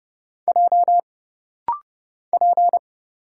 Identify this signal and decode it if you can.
Morse code, heard as a keyed tone, about 30 wpm, 700 Hz. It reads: J P
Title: Single Letter-Number - Rapid-Fire 30wpm